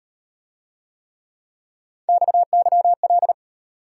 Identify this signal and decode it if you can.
Morse code, heard as a keyed tone, about 33 wpm, 700 Hz. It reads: XYL